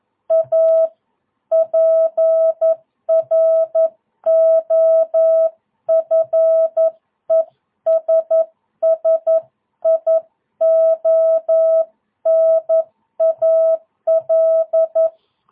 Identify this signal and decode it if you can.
Morse code, heard as a keyed tone, about 11 wpm, 650 Hz. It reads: A PROFESSIONAL
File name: jh8cbh / 2022cw